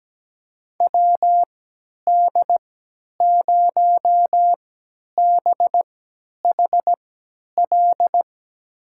Morse code, heard as a keyed tone, about 17 wpm, spaced as for 10 wpm, 700 Hz. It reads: WD0BHL
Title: USA Call Signs - 3x Character Spacing - 17wpm